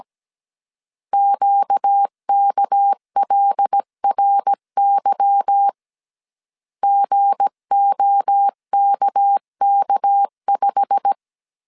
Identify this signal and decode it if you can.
Morse code, heard as a keyed tone, about 17 wpm, 800 Hz. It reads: QKLRY GOKK5
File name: 1500C S B 16wpm